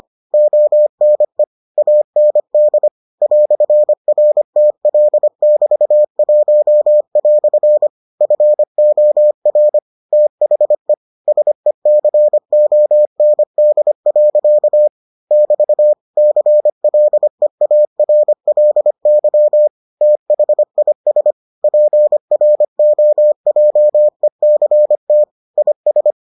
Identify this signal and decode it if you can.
Morse code, heard as a keyed tone, about 25 wpm, 600 Hz. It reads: ONE AND "RTL<BT>1" FOR THE SECOND. <BT> CLEARLY THIS PROJECT IS